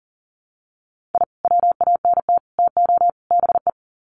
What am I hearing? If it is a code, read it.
Morse code, heard as a keyed tone, about 40 wpm, 700 Hz. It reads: I WANT TO BE
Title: Sets of 4 Words 40wpm